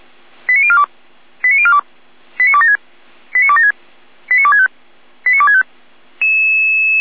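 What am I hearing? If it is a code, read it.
Five-tone selective call (ZVEI), heard as quick runs of five tones, tones of about 70 ms. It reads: 89042 89042 89276 89276 89265 89265
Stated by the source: Five-tone ZVEI selective call 89294 received on an SDR.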